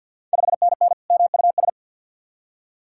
Morse code, heard as a keyed tone, about 53 wpm, 700 Hz. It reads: 5NN MUS
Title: Ontario QSO Party Exchange 50wpm